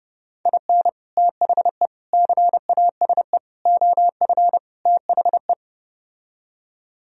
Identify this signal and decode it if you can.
Morse code, heard as a keyed tone, about 33 wpm, 700 Hz. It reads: IN THE CASE OF THE